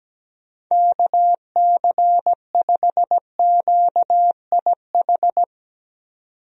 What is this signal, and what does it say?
Morse code, heard as a keyed tone, about 17 wpm, 700 Hz. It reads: KC5QIH
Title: USA Call Signs - 6 Characters - 17wpm